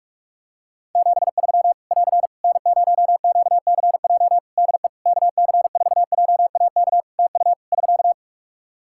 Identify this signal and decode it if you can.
Morse code, heard as a keyed tone, about 46 wpm, 700 Hz. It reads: 73 <AR> N0QCJ DE KC4JAK TU <SK>
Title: K1USN SST QSO Practice 45wpm